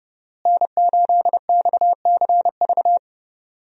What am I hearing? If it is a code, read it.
Morse code, heard as a keyed tone, about 30 wpm, 700 Hz. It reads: N8XCV
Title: USA Call Signs - 5 Characters - Repeated 3x - 30wpm